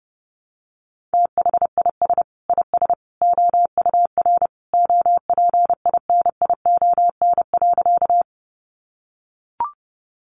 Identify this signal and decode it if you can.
Morse code, heard as a keyed tone, about 30 wpm, 700 Hz. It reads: THIS IS OUR OPINION.
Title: Sentences from Top 1200 Words 30wpm